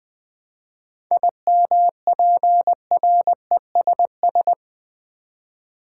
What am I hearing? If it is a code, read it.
Morse code, heard as a keyed tone, about 20 wpm, 700 Hz. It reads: IMPRESS
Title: Seven Letter Words - 20wpm